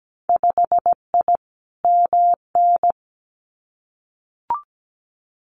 Morse code, heard as a keyed tone, about 17 wpm, 700 Hz. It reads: 5I MN
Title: Winter Field Day Exchanges Encore 17wpm